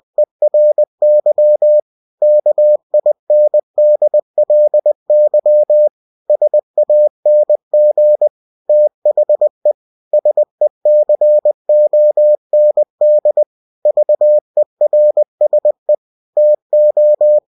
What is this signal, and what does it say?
Morse code, heard as a keyed tone, about 20 wpm, 600 Hz. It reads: ERY KINDLY SANG THE SECOND VERSE TO
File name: cw-books / winnie-the-pooh / winnie-20wpm-page-0045